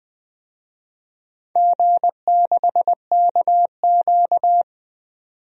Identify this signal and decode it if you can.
Morse code, heard as a keyed tone, about 20 wpm, 700 Hz. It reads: G6KQ
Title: DXCC-100 Call Signs - 4 Characters - 20wpm